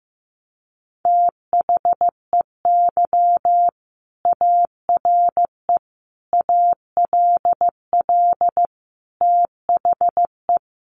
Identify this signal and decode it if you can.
Morse code, heard as a keyed tone, about 15 wpm, 700 Hz. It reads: THEY ARE ALL THE
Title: Sentences from Top 600 Words 15wpm